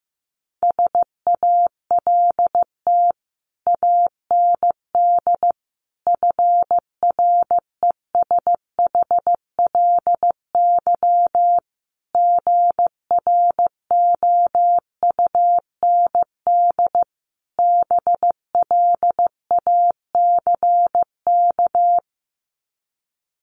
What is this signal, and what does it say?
Morse code, heard as a keyed tone, about 15 wpm, 700 Hz. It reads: SALT AND FRESHLY GROUND BLACK
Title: Sets of 5 Words Encore 15wpm